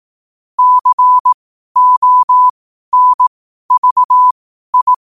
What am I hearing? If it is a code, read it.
Morse code, heard as a keyed tone, about 18 wpm, spaced as for 12 wpm, 1000 Hz. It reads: CONVI